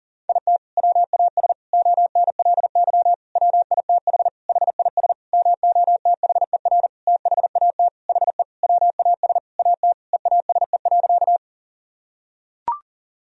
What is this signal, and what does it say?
Morse code, heard as a keyed tone, about 40 wpm, 700 Hz. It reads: IT WAS ONLY WITH HIS MOTHER THAT HE WAS AT EASE.
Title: Sentences from Top 1000 Words - Review Part 1 - 40wpm